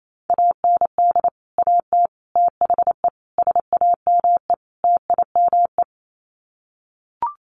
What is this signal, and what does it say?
Morse code, heard as a keyed tone, about 28 wpm, 700 Hz. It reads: AND AT THE SAME TIME
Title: Sets of 5 Words 28wpm